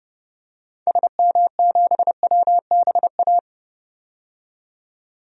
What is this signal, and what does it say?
Morse code, heard as a keyed tone, about 30 wpm, 700 Hz. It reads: SM7WBA
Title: DXCC-100 Call Signs - 6 Characters - 30wpm